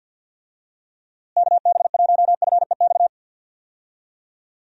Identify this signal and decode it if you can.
Morse code, heard as a keyed tone, about 53 wpm, 700 Hz. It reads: KB1FEX
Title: USA Call Signs - 6 Characters - Repeated 2x - 50wpm